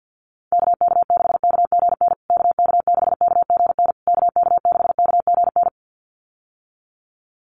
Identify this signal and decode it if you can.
Morse code, heard as a keyed tone, about 50 wpm, 700 Hz. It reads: KK6KGN KK6KGN KK6KGN
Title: USA Call Signs Repeated 3x 50wpm